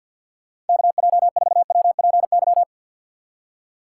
Morse code, heard as a keyed tone, about 54 wpm, 700 Hz. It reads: KJ4WPY
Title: USA Call Signs - 6 Characters - 8x Character Spacing - 50wpm